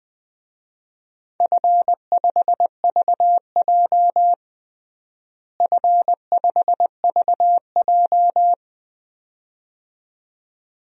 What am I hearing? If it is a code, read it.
Morse code, heard as a keyed tone, about 20 wpm, 700 Hz. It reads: F5VJ F5VJ
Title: DXCC-100 Call Signs - 4 Characters - Repeated 2x - 20wpm